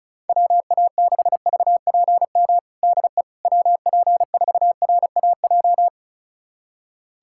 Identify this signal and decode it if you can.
Morse code, heard as a keyed tone, about 36 wpm, 700 Hz. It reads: WA6VPM DE WP4RAJ
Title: USA Callsign DE Callsign - 6 Characters - 35wpm